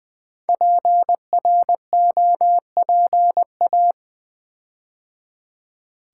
Morse code, heard as a keyed tone, about 20 wpm, 700 Hz. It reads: PROPA
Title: Build A Word - 5 Letters At A Time - 10-Letter Words - 20wpm